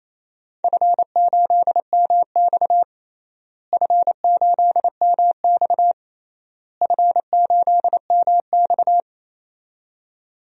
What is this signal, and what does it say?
Morse code, heard as a keyed tone, about 28 wpm, 700 Hz. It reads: F8MX F8MX F8MX
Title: DXCC-100 Call Signs - 4 Characters - Repeated 3x - 28wpm